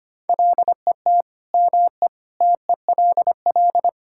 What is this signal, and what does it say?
Morse code, heard as a keyed tone, about 25 wpm, 700 Hz. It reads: LET ME TELL
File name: practice / 25wpm / Sentences with 4-Word Phrases - 25wpm